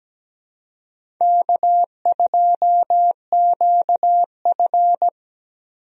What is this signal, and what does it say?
Morse code, heard as a keyed tone, about 17 wpm, 700 Hz. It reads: K2QF
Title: USA Call Signs - 4 Characters - Repeated 2x - 17wpm